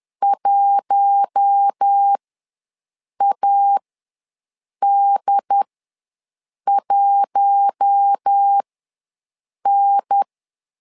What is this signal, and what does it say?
Morse code, heard as a keyed tone, about 10 wpm, spaced as for 6 wpm, 800 Hz. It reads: 1AD1N
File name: LEZIONI / 4 Q1 ws 10wpm 2538